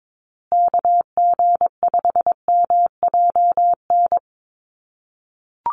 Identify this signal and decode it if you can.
Morse code, heard as a keyed tone, about 22 wpm, 700 Hz. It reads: KG5MJN
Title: USA Call Signs Repeated 2x 22wpm